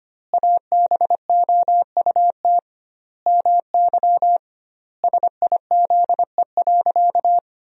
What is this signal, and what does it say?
Morse code, heard as a keyed tone, about 25 wpm, 700 Hz. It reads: ABOUT MY SIZE.
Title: 2x Word-Spacing - Sentences from Top 400 Words - 25wpm